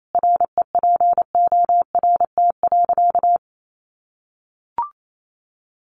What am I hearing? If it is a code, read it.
Morse code, heard as a keyed tone, about 28 wpm, 700 Hz. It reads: REPORT.
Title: Sentences from Top 1300 Words 28wpm